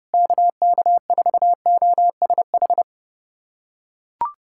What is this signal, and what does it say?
Morse code, heard as a keyed tone, about 30 wpm, 700 Hz. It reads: KK4OSH
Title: USA Callsign DE Callsign 30wpm